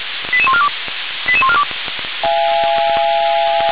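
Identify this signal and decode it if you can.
Five-tone selective call (ZVEI), heard as quick runs of five tones, tones of about 70 ms. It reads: 88142 88142
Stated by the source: ZVEI five-tone selective call received on SDR.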